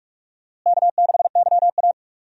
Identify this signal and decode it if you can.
Morse code, heard as a keyed tone, about 43 wpm, 700 Hz. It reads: K6YA
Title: USA Call Signs - 4 Characters - Repeated 2x - 45wpm